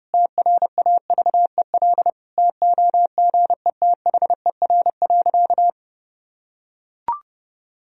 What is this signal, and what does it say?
Morse code, heard as a keyed tone, about 30 wpm, 700 Hz. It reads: TRAVEL TOGETHER.
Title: Sentences from Top 1000 Words - Review Part 2 - 30wpm